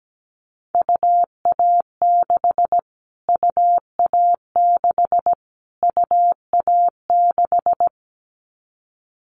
Morse code, heard as a keyed tone, about 17 wpm, 700 Hz. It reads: UA6 UA6 UA6
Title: Callsign Prefixes for Top 25 DXCC - Repeated 3x - 17wpm